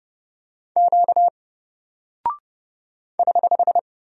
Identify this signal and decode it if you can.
Morse code, heard as a keyed tone, about 30 wpm, 700 Hz. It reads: Q <HH>
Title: Single Character (Letter:Number:Punctuation:Prosign) 30wpm